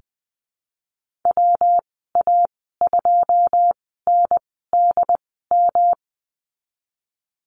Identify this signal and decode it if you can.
Morse code, heard as a keyed tone, about 20 wpm, spaced as for 14 wpm, 700 Hz. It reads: WA2NDM
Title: 179 Lesson 29g - Slash - Call Signs 20wpm